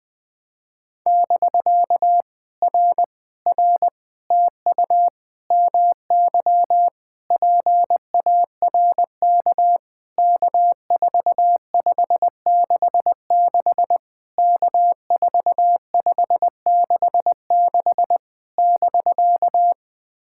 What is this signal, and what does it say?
Morse code, heard as a keyed tone, about 19 wpm, 700 Hz. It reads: <BK> R R TU MY PARK K4566 K4566 <BK>
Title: POTA QSO Practice - 20wpm